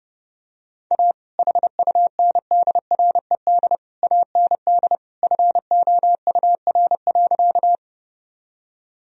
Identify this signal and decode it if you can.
Morse code, heard as a keyed tone, about 30 wpm, 700 Hz. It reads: A HUNDRED AND FOUR.
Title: Sentences from Top 1000 Words - Review Part 2 - 30wpm